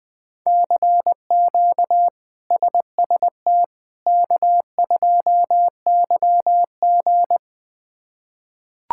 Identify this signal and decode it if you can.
Morse code, heard as a keyed tone, about 20 wpm, 700 Hz. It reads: CQ SST K2YG
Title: K1USN SST QSO Practice - 20wpm